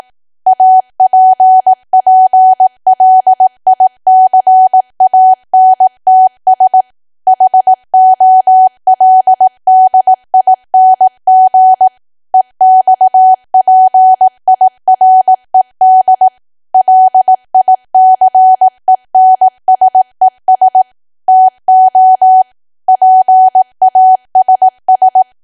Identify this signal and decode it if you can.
Morse code, heard as a keyed tone, about 18 wpm, 750 Hz. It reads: APPLICANTS HOLDING EXPIRED LICENSES TO PASS
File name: cwa / W1AW-Level2 / 18WPM